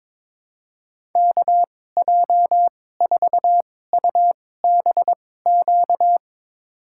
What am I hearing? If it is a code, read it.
Morse code, heard as a keyed tone, about 22 wpm, spaced as for 16 wpm, 700 Hz. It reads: KJ4UBQ
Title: USA Call Signs - 6 Characters - 2x Character Spacing - 22wpm